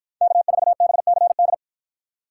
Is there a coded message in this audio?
Yes: Morse code from 0.2 to 1.5 s, heard as a keyed tone, about 50 wpm, 700 Hz. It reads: K4BCD